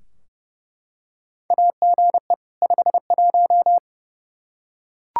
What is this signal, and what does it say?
Morse code, heard as a keyed tone, about 30 wpm, 700 Hz. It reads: AGE 51
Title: Practice QSO - Exchange 3 - More Info - 30wpm